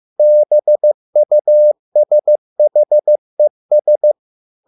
Morse code, heard as a keyed tone, about 15 wpm, 600 Hz. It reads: BUSHES